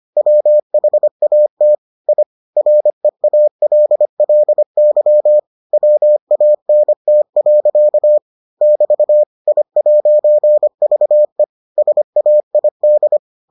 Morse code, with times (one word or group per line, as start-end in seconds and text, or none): 0.2-1.7 WHAT
2.1-2.2 I
2.6-5.4 REALLY
5.7-8.2 WANT.
8.6-9.2 <BT>
9.5-11.4 I'VE
11.8-13.2 SAID